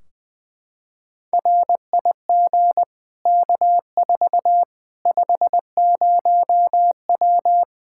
Morse code, heard as a keyed tone, about 20 wpm, 700 Hz. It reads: RIG K4 50W
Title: Practice QSO - Exchange 2 - Info - 20wpm